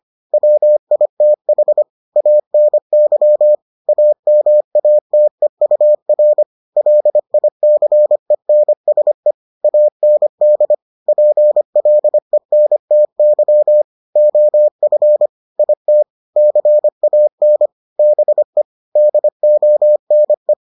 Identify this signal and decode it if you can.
Morse code, heard as a keyed tone, about 25 wpm, 600 Hz. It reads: WITH ANY AMATEUR LICENSE AND PLENTY OF IT CAN BE DONE